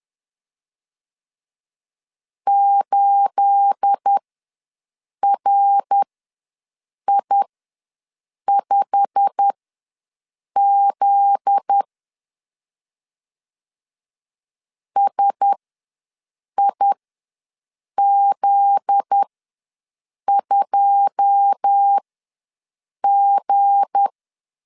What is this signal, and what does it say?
Morse code, heard as a keyed tone, about 10 wpm, spaced as for 6 wpm, 800 Hz. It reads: 8RI5Z SIZ2G
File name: LEZIONI / 9 28 ws 10wpm 4124